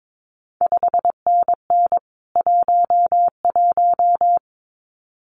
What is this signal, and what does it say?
Morse code, heard as a keyed tone, about 22 wpm, 700 Hz. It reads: 5NN 11
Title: CQ WW DX Contest Exchange Encore - 22wpm